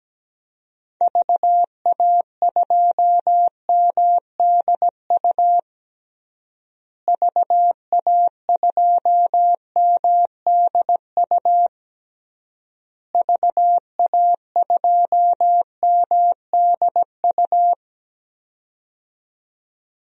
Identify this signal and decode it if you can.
Morse code, heard as a keyed tone, about 17 wpm, 700 Hz. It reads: VA2MDU VA2MDU VA2MDU
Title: DXCC-100 Call Signs - 6 Characters - Repeated 3x - 17wpm